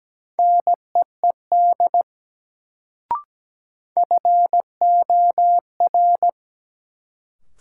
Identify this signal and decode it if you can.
Morse code, heard as a keyed tone, about 17 wpm, 700 Hz. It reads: NEED FOR
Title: Sets of 2 Words - New Word Drill - 17wpm